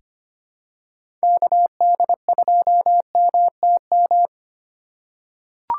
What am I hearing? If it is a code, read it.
Morse code, heard as a keyed tone, about 25 wpm, 700 Hz. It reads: KD2MTM
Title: USA Call Signs - 2x Character Spacing - 25wpm